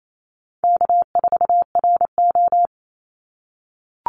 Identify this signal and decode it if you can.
Morse code, heard as a keyed tone, about 28 wpm, 700 Hz. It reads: K4RO
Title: Call Signs - 28wpm